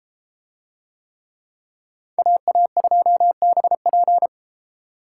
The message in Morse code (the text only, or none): AA2BP